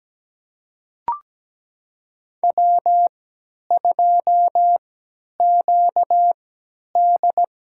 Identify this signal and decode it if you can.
Morse code, heard as a keyed tone, about 17 wpm, spaced as for 10 wpm, 700 Hz. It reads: W2QD
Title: USA Call Signs - 3x Character Spacing - 17wpm